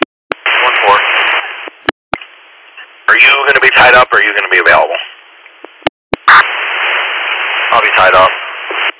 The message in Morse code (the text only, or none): none